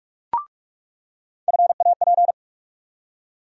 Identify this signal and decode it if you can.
Morse code, heard as a keyed tone, about 46 wpm, 700 Hz. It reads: FAP
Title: USA Call Sign Postfixes - 3 Letters 45wpm